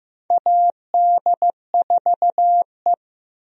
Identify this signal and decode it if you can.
Morse code, heard as a keyed tone, about 15 wpm, 700 Hz. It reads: AD4E
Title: USA Call Signs - 4 Characters - Repeated 2x - 15wpm